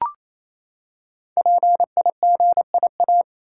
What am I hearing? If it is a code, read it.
Morse code, heard as a keyed tone, about 28 wpm, 700 Hz. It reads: PIGIA
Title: Code Groups - 5 Letters - 28wpm